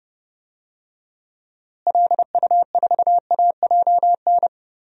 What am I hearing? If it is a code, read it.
Morse code, heard as a keyed tone, about 32 wpm, 700 Hz. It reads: LU4AJN
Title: DXCC-100 Call Signs - 6 Characters - Repeated 2x - 30wpm